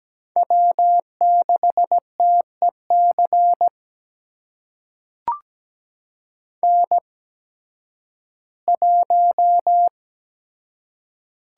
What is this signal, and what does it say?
Morse code, heard as a keyed tone, about 17 wpm, spaced as for 11 wpm, 700 Hz. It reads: W6TEC N1
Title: USA Call Signs - 5 Characters - 8x Character Spacing - 17wpm